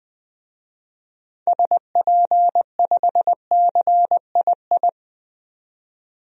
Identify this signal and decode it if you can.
Morse code, heard as a keyed tone, about 20 wpm, 700 Hz. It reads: SP5CII
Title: DXCC-100 Call Signs - 6 Characters - 20wpm